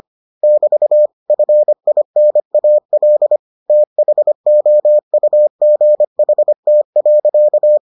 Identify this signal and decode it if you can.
Morse code, heard as a keyed tone, about 25 wpm, 600 Hz. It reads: <BT> FINAL THOUGHT.